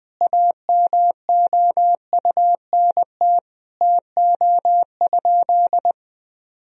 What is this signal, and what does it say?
Morse code, heard as a keyed tone, about 20 wpm, 700 Hz. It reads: AMOUNT TO?
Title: Sentences from Top 1000 Words - Review Part 4 - 20wpm